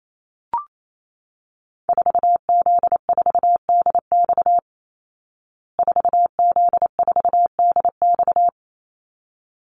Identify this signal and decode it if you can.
Morse code, heard as a keyed tone, about 28 wpm, 700 Hz. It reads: 4Z4DX 4Z4DX
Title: Call Signs Repeated 3x - 28wpm